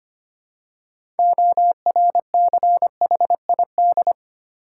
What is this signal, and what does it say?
Morse code, heard as a keyed tone, about 25 wpm, 690 Hz. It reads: ORCHID
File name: practice / 25wpm / Easy Words with Hints - 25wpm